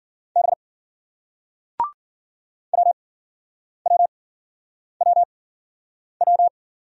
Morse code, heard as a keyed tone, about 50 wpm, 700 Hz. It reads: B W W W W
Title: Speed-Racing Single Letters 40wpm